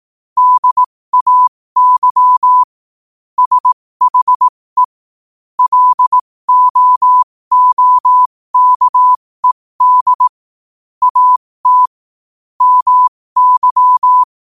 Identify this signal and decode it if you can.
Morse code, heard as a keyed tone, about 18 wpm, spaced as for 15 wpm, 1000 Hz. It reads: DAY SHE LOOKED AT MY
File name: qotdfiles / 15wpm